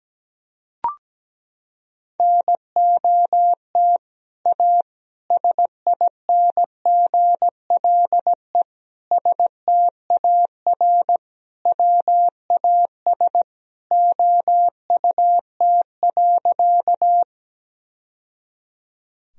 Morse code, heard as a keyed tone, about 17 wpm, 700 Hz. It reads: NOT A SINGLE STAR WAS OUT.
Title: Sentences from Top 1000 Words - Review Part 4 - 17wpm